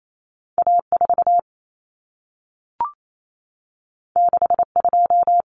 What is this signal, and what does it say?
Morse code, heard as a keyed tone, about 28 wpm, 700 Hz. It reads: A4 62
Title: Common Cut Numbers - 2-Digit 28wpm